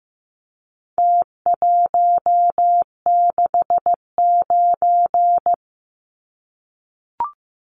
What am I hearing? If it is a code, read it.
Morse code, heard as a keyed tone, about 15 wpm, 700 Hz. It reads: T169
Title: Numbers - Contest Style - 4-Digit 15wpm